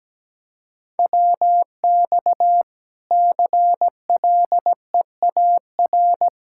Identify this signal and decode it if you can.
Morse code, heard as a keyed tone, about 17 wpm, 700 Hz. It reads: WX CLEAR
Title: Practice QSO - Exchange 2 - Info - 17wpm